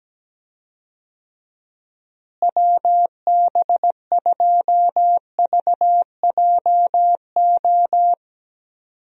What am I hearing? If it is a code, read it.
Morse code, heard as a keyed tone, about 17 wpm, 700 Hz. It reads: WB2VJO
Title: USA Call Signs - 6 Characters - Repeated 2x - 17wpm